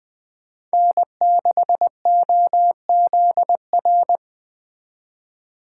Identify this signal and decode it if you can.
Morse code, heard as a keyed tone, about 20 wpm, 700 Hz. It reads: N6OZR